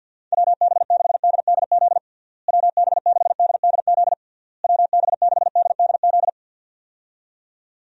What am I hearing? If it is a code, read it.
Morse code, heard as a keyed tone, about 50 wpm, 700 Hz. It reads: WB6DDZ WB6DDZ WB6DDZ
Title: USA Call Signs - 6 Characters - Repeated 3x - 50wpm